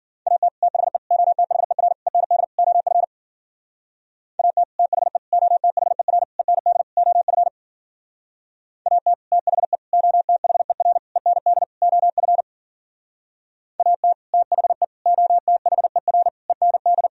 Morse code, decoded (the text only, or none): AT THE OTHER END OF AT THE OTHER END OF AT THE OTHER END OF AT THE OTHER END